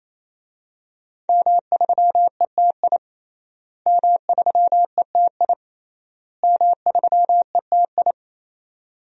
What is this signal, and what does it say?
Morse code, heard as a keyed tone, about 28 wpm, 700 Hz. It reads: M3ETI M3ETI M3ETI
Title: DXCC-100 Call Signs - 5 Characters - Repeated 3x - 28wpm